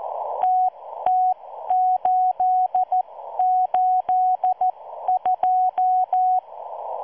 Morse code, heard as a keyed tone, about 14 wpm, spaced as for 12 wpm, 740 Hz. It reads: TT882